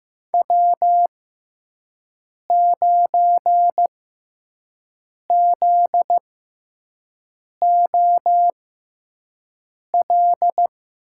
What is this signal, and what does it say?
Morse code, heard as a keyed tone, about 15 wpm, spaced as for 5 wpm, 700 Hz. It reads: W9ZOL
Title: USA Call Signs - 5 Characters - 6x Character Spacing - 15wpm